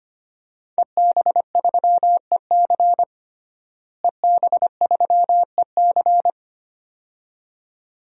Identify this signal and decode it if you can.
Morse code, heard as a keyed tone, about 25 wpm, 700 Hz. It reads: EB3EC EB3EC
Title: DXCC-100 Call Signs - 5 Characters - Repeated 2x - 25wpm